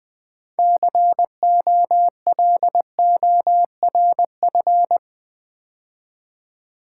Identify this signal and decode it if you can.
Morse code, heard as a keyed tone, about 20 wpm, 700 Hz. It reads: COLORF
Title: Build A Word - 3 Letters At A Time - 8-Letter Words - 20wpm